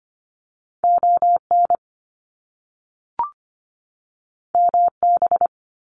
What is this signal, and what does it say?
Morse code, heard as a keyed tone, about 25 wpm, 700 Hz. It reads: ON MB
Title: Canadian Provinces and Territories 25wpm